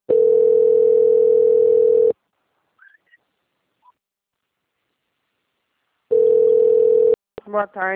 Ringback tone (North America), 440 + 480 Hz.
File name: out- - - - - .14173